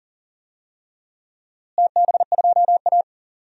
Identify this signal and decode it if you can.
Morse code, heard as a keyed tone, about 40 wpm, 700 Hz. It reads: TB2A
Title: DXCC-100 Call Signs - 4 Characters - Repeated 3x - 40wpm